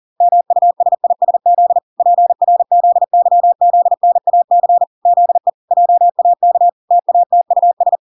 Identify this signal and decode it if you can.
Morse code, heard as a keyed tone, about 40 wpm, 700 Hz. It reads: MUSISZ PRZYZNAC ZE JAK TATUS